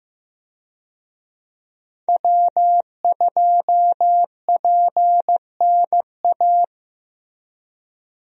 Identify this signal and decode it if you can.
Morse code, heard as a keyed tone, about 15 wpm, 700 Hz. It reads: W2PNA
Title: USA Call Signs - 5 Characters - Repeated 2x - 15wpm